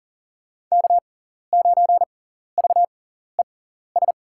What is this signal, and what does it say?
Morse code, heard as a keyed tone, about 37 wpm, spaced as for 14 wpm, 700 Hz. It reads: K9VES
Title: USA Call Signs - 5 Characters - 6x Character Spacing - 40wpm